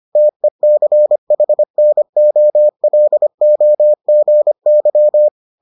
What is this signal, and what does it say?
Morse code, heard as a keyed tone, about 25 wpm, 600 Hz. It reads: TECHNOLOGY